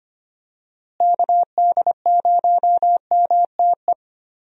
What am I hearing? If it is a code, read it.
Morse code, heard as a keyed tone, about 25 wpm, 700 Hz. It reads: KD0MTE
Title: USA Call Signs - 6 Characters - Repeated 2x - 25wpm